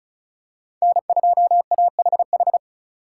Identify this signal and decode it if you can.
Morse code, heard as a keyed tone, about 34 wpm, 700 Hz. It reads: N2AHH
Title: USA Call Signs - 5 Characters - 4x Character Spacing - 35wpm